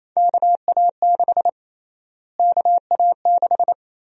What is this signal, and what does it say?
Morse code, heard as a keyed tone, about 28 wpm, 700 Hz. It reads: KA6 KA6
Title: USA Call Sign Prefixes - Repeated 2x - 28wpm